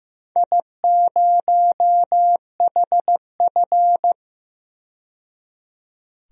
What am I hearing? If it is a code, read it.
Morse code, heard as a keyed tone, about 15 wpm, 700 Hz. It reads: I0HF
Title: DXCC-100 Call Signs - 4 Characters - 15wpm